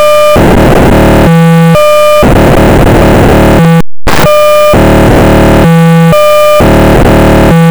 POCSAG pager data.